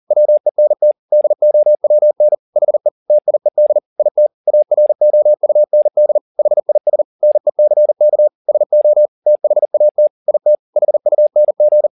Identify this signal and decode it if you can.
Morse code, heard as a keyed tone, about 40 wpm, 600 Hz. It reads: WENT DOWN HE TIED IT AROUND HIS NECK SO THAT IT HUNG